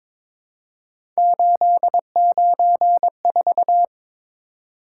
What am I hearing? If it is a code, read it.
Morse code, heard as a keyed tone, about 22 wpm, 700 Hz. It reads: 894
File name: practice / 22wpm / Three Letter-Number Combinations - 22wpm